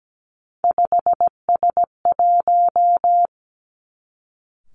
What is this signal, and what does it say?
Morse code, heard as a keyed tone, about 17 wpm, 700 Hz. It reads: 5S1
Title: Three Challenging Letter-Number Combinations 17wpm